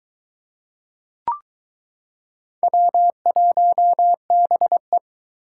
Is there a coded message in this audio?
Yes: Morse code from 2.6 to 5.0 s, heard as a keyed tone, about 22 wpm, 700 Hz. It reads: W1BE